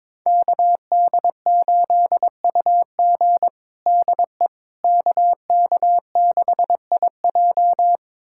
Morse code, heard as a keyed tone, about 22 wpm, 700 Hz. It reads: KD8UG DE KK6IJ
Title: USA Callsign DE Callsign - 5 Characters - 22wpm